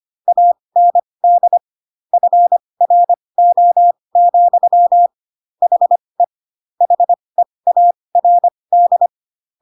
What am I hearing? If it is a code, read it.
Morse code, heard as a keyed tone, about 25 wpm, spaced as for 20 wpm, 700 Hz. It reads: AND FRO, HE HEARD
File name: Pinocchio / Pinocchio ch1 15 25 ik1hge